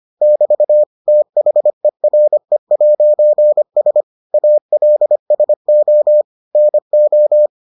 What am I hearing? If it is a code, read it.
Morse code, heard as a keyed tone, about 25 wpm, 600 Hz. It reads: <BT> THERE'S ALSO NO